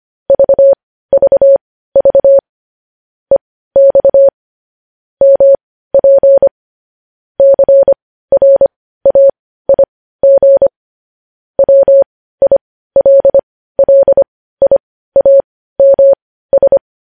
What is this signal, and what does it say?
Morse code, heard as a keyed tone, about 25 wpm, spaced as for 15 wpm, 550 Hz. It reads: VVV EX MP CRAIG WILLIAMS
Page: News Headlines in Morse Code at 15 WPM